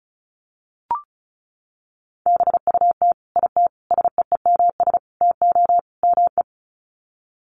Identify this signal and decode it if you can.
Morse code, heard as a keyed tone, about 34 wpm, 700 Hz. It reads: BUT IT SEEMS TO ME
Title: Sets of 5 Words Encore2 35wpm